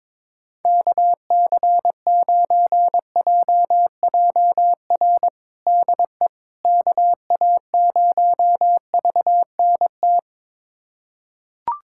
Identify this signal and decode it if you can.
Morse code, heard as a keyed tone, about 22 wpm, 700 Hz. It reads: KC9JJR DE KA0VNT
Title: USA Callsign DE Callsign - 6 Characters - 22wpm